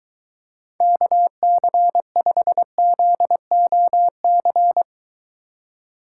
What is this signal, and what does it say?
Morse code, heard as a keyed tone, about 23 wpm, 700 Hz. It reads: KC5ZOC